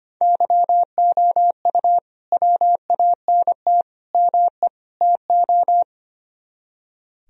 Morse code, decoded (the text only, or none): YOU WANT ME TO